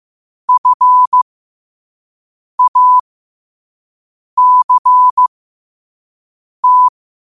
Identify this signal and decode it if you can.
Morse code, heard as a keyed tone, about 15 wpm, spaced as for 5 wpm, 1000 Hz. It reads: FACT
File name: qotdfiles / 5wpm